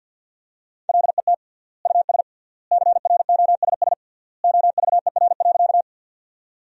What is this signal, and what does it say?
Morse code, heard as a keyed tone, about 47 wpm, 700 Hz. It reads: LET US CROSS OVER.
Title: 3x Word-Spacing - Sentences from Top 300 Words - 50wpm